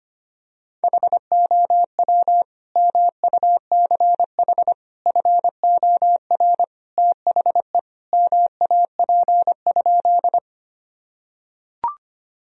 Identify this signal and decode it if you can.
Morse code, heard as a keyed tone, about 25 wpm, 700 Hz. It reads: HOW MUCH FOR THE MAP?
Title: Sentences from Top 1000 Words - Review Part 2 - 25wpm